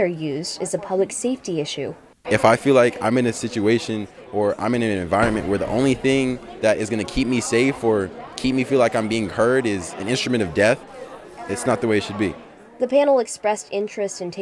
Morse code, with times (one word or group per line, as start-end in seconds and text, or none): none